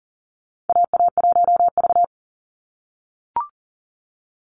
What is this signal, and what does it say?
Morse code, heard as a keyed tone, about 40 wpm, 700 Hz. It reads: AA1V
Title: Call Signs - 40wpm